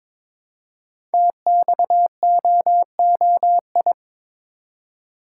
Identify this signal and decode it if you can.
Morse code, heard as a keyed tone, about 22 wpm, 700 Hz. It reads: TXOOI
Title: Code Groups - 5 Letters - 22wpm